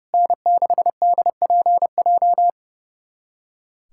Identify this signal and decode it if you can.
Morse code, heard as a keyed tone, about 30 wpm, 700 Hz. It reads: N6DPJ
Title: USA Callsign DE Callsign - 5 Characters - 30wpm